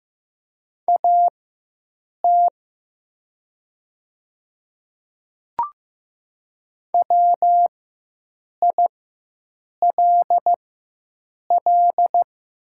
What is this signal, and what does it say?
Morse code, heard as a keyed tone, about 15 wpm, spaced as for 7 wpm, 700 Hz. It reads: AT WILL
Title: 4x Character Spacing - Top 100 Words - 15wpm